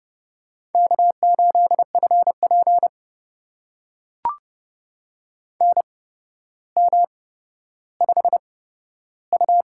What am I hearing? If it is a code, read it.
Morse code, heard as a keyed tone, about 30 wpm, spaced as for 13 wpm, 700 Hz. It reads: K8FP NM5U
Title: USA Call Signs - 4 Characters - 8x Character Spacing - 30wpm